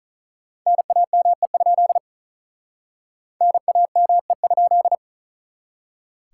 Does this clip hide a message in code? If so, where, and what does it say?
Yes: Morse code from 0.7 to 5.0 s, heard as a keyed tone, about 49 wpm, 700 Hz. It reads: NAME? NAME?